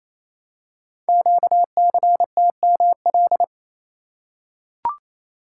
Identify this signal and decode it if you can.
Morse code, heard as a keyed tone, about 28 wpm, 690 Hz. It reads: QCTML
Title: Code Groups - 5 Letters - 28wpm